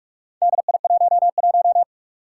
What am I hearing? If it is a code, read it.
Morse code, heard as a keyed tone, about 44 wpm, 700 Hz. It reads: DI11
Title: Four Challenging Letter-Number Combinations 45wpm